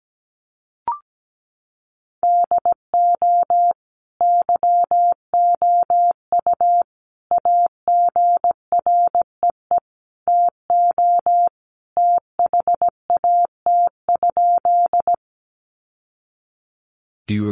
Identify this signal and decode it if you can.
Morse code, heard as a keyed tone, about 17 wpm, 700 Hz. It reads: DO YOU AGREE TO THAT?
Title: Sentences from Top 1100 Words 17wpm